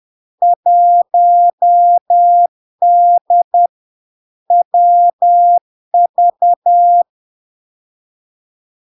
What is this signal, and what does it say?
Morse code, heard as a keyed tone, about 10 wpm, 700 Hz. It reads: 1D WV
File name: LARK-FD-2022-Log-in-CW-10wpm-700-Hz